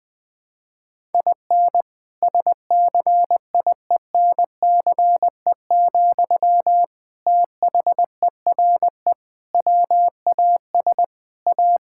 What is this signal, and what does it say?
Morse code, heard as a keyed tone, about 21 wpm, 700 Hz. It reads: IN SCIENCE, THERE WAS A
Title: Sentences from Top 1000 Words - Review Part 5 - 20wpm